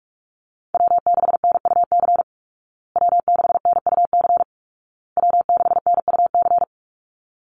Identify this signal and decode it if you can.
Morse code, heard as a keyed tone, about 44 wpm, 700 Hz. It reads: W6NUC W6NUC W6NUC
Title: Call Signs Repeated 2x - 45wpm